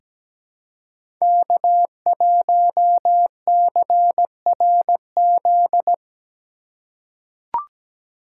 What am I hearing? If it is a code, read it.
Morse code, heard as a keyed tone, about 17 wpm, 700 Hz. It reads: K1CRZ
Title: USA Call Signs - 5 Characters - 3x Character Spacing - 17wpm